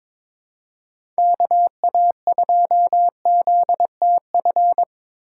Morse code, heard as a keyed tone, about 21 wpm, 700 Hz. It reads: KA2ZTF
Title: USA Call Signs - 6 Characters - 3x Character Spacing - 22wpm